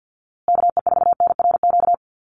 Morse code, heard as a keyed tone, about 49 wpm, 700 Hz. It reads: KE4NRQ
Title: USA Call Signs Repeated 2x 50wpm